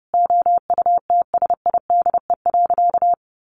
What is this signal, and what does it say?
Morse code, heard as a keyed tone, about 30 wpm, 700 Hz. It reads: OUTSIDE.
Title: Sentences from Top 900 Words 30wpm